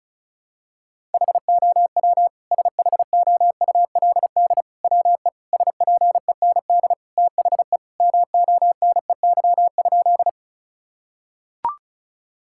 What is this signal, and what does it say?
Morse code, heard as a keyed tone, about 36 wpm, 700 Hz. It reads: HOW SHOULD WE SPEND THE MONEY?
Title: Sentences from Top 1000 Words - Review Part 5 - 35wpm